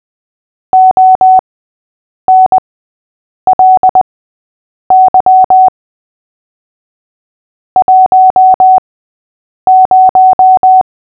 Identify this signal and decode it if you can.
Morse code, heard as a keyed tone, about 20 wpm, spaced as for 8 wpm, 750 Hz. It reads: ONLY 10
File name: cw / fists / Short-Story-5-8wpm